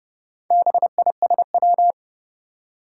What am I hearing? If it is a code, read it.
Morse code, heard as a keyed tone, about 30 wpm, 700 Hz. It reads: BISW
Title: Four Challenging Letter-Number Combinations 30wpm